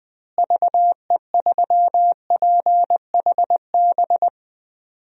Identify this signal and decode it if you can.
Morse code, heard as a keyed tone, about 20 wpm, 700 Hz. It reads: VE3PHB
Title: DXCC-100 Call Signs - 6 Characters - Repeated 3x - 20wpm